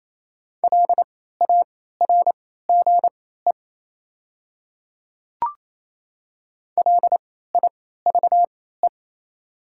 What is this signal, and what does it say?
Morse code, heard as a keyed tone, about 28 wpm, spaced as for 16 wpm, 700 Hz. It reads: LARGE LIVE
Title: 3x Character Spacing - Top 300 Words - 28wpm